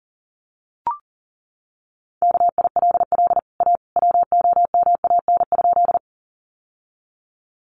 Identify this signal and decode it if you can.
Morse code, heard as a keyed tone, about 40 wpm, 700 Hz. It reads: KILL A WOMAN?
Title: Sentences from Top 1400 Words 40wpm